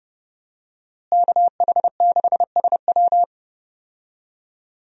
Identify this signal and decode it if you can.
Morse code, heard as a keyed tone, about 29 wpm, 700 Hz. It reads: KH6SW
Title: DXCC-100 Call Signs - 5 Characters - Repeated 2x - 30wpm